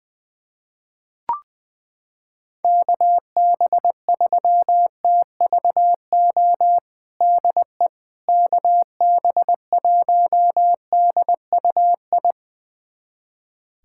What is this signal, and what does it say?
Morse code, heard as a keyed tone, about 20 wpm, 700 Hz. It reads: KB3TVO DE KB1DUI
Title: USA Callsign DE Callsign - 6 Characters - 20wpm